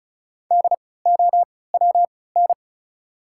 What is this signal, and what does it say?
Morse code, heard as a keyed tone, about 34 wpm, spaced as for 20 wpm, 690 Hz. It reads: DOWN
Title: 3x Character Spacing - Top 200 Words - 35wpm